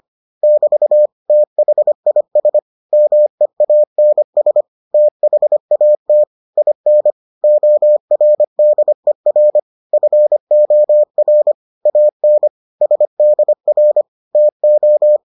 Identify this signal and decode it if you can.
Morse code, heard as a keyed tone, about 25 wpm, 600 Hz. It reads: <BT> THIS MEANS THAT IN ORDER FOR AN SDR TO